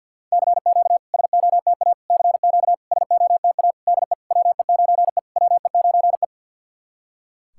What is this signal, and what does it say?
Morse code, heard as a keyed tone, about 48 wpm, 690 Hz. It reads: CQ SOTA CQ SOTA DE WE9E WE9E
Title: SOTA QSO Practice 50wpm